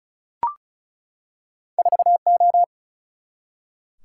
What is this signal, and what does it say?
Morse code, heard as a keyed tone, about 36 wpm, 700 Hz. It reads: 4O
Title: Single-Letter Single-Number 35wpm